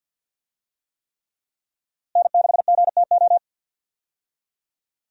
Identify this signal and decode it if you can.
Morse code, heard as a keyed tone, about 50 wpm, 700 Hz. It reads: N6GTO
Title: USA Call Signs - 5 Characters - Repeated 3x - 50wpm